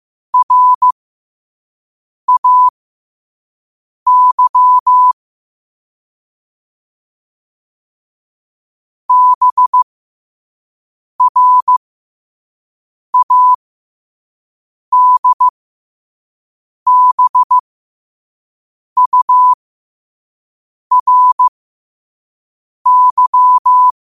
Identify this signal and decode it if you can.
Morse code, heard as a keyed tone, about 15 wpm, spaced as for 5 wpm, 1000 Hz. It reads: RAY BRADBURY
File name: qotdfiles / 5wpm